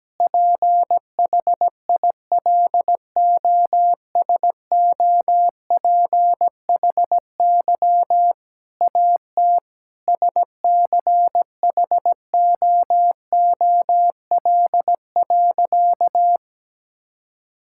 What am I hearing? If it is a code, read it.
Morse code, heard as a keyed tone, about 17 wpm, 700 Hz. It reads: PHILOSOPHY AT SCHOOL.
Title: Sentences from Top 1000 Words - Review Part 6 - 17wpm